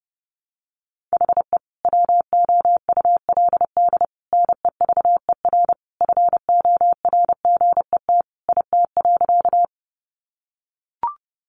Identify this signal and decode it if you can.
Morse code, heard as a keyed tone, about 30 wpm, 700 Hz. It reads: HE WOULD NEVER FORGET IT.
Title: Sentences from Top 1200 Words 30wpm